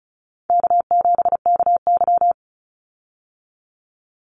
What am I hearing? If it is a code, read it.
Morse code, heard as a keyed tone, about 34 wpm, 700 Hz. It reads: K7KY
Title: Call Signs Repeated 3x - 35wpm